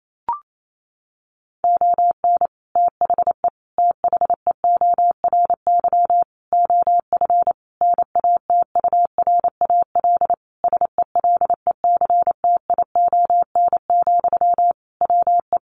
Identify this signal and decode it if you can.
Morse code, heard as a keyed tone, about 28 wpm, 700 Hz. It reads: ON THE THEORY OF NATURAL SELECTION, WE
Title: Sentences from Top 1400 Words 28wpm